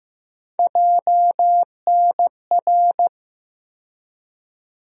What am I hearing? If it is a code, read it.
Morse code, heard as a keyed tone, about 15 wpm, 700 Hz. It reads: JNR
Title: USA Call Sign Postfixes - 3 Letters 15wpm